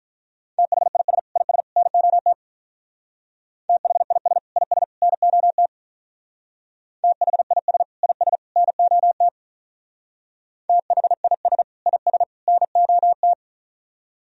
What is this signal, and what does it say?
Morse code, heard as a keyed tone, about 45 wpm, 700 Hz. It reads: THIS IS NOT THIS IS NOT THIS IS NOT THIS IS NOT